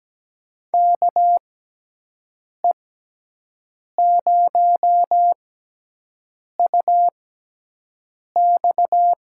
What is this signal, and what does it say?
Morse code, heard as a keyed tone, about 17 wpm, spaced as for 6 wpm, 700 Hz. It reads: KE0UX
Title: USA Call Signs - 6x Character Spacing - 17wpm